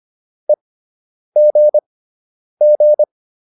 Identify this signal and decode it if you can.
Morse code, heard as a keyed tone, about 25 wpm, spaced as for 9 wpm, 600 Hz. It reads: EGG